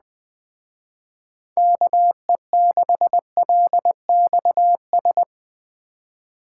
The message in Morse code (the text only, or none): KE6LXS